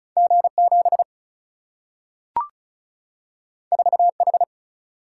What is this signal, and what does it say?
Morse code, heard as a keyed tone, about 36 wpm, 700 Hz. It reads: G 7 4 H
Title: Single-Letter Single-Number 35wpm